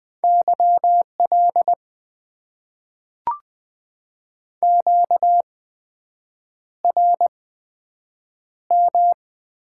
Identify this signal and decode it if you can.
Morse code, heard as a keyed tone, about 20 wpm, spaced as for 7 wpm, 700 Hz. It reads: YL QRM
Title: Top 90 QSO Elements - 8x Character Spacing - 20wpm